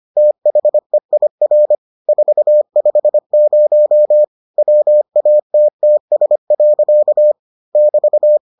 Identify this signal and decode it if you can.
Morse code, heard as a keyed tone, about 25 wpm, 600 Hz. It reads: THEIR 450 WATTS. <BT>